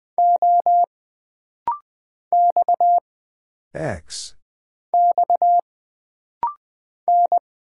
Morse code, heard as a keyed tone, about 20 wpm, 700 Hz. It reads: O X X N
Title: Single Character (Letter:Number:Punctuation:Prosign) - Rapid-Fire - 20wpm